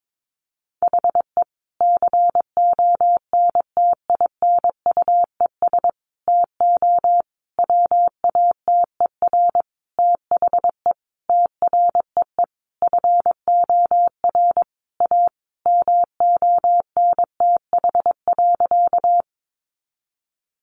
Morse code, heard as a keyed tone, about 22 wpm, 700 Hz. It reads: HE CONTINUES TO WATER THE TREE FOR A MONTH.
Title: Easy Phrases 22wpm